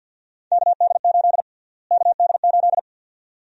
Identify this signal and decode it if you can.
Morse code, heard as a keyed tone, about 53 wpm, 700 Hz. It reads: KD8 KD8
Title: USA Call Sign Prefixes - Repeated 2x - 50wpm